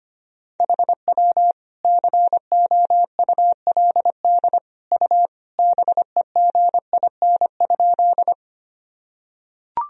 Morse code, heard as a keyed tone, about 25 wpm, 700 Hz. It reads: HW COULD U BEGIN?
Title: Sentences from Top 300 Words and Abbreviations 25wpm